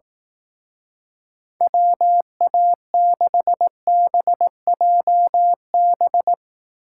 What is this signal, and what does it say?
Morse code, heard as a keyed tone, about 18 wpm, 700 Hz. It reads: WA6BJB
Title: Speed Racing - USA Call Signs 15wpm